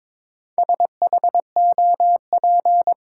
Morse code, heard as a keyed tone, about 21 wpm, 700 Hz. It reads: SHOP
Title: Sentences from Top 1600 Words - New Word Drill Alt - 22wpm